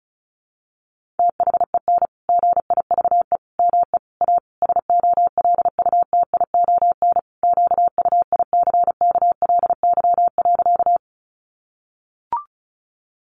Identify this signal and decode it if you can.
Morse code, heard as a keyed tone, about 34 wpm, 700 Hz. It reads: THEN GIVE ME A SOLUTION QUICKLY.
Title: Sentences from Top 1300 Words 35wpm